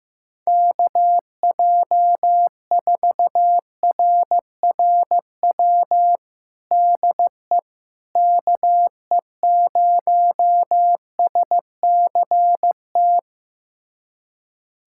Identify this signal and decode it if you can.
Morse code, heard as a keyed tone, about 15 wpm, 700 Hz. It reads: KJ4RRW DE KE0SCT
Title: USA Callsign DE Callsign - 6 Characters - 15wpm